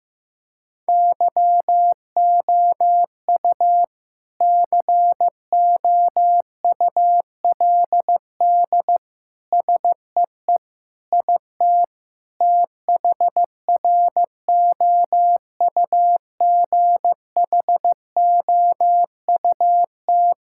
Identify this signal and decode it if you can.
Morse code, heard as a keyed tone, about 15 wpm, 700 Hz. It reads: YOU COULD SEE IT THROUGHOUT
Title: Sentences from Top 1000 Words - Review Part 5 - 15wpm